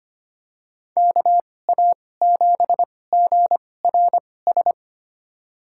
Morse code, heard as a keyed tone, about 25 wpm, spaced as for 18 wpm, 700 Hz. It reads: KA7GRS
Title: USA Call Signs - 2x Character Spacing - 25wpm